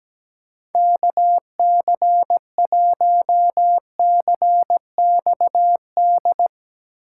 Morse code, heard as a keyed tone, about 17 wpm, 700 Hz. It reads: KC1CXD
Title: USA Call Signs - 6 Characters - 3x Character Spacing - 17wpm